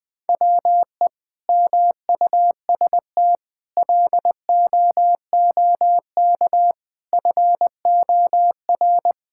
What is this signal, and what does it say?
Morse code, heard as a keyed tone, about 21 wpm, 700 Hz. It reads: WE MUST LOOK FOR
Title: Sentences from Top 1600 Words 20wpm